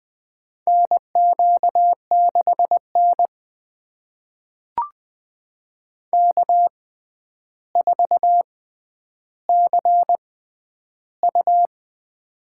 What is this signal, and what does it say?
Morse code, heard as a keyed tone, about 20 wpm, spaced as for 10 wpm, 700 Hz. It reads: NQ6N K4CU
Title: USA Call Signs - 4 Characters - 6x Character Spacing - 20wpm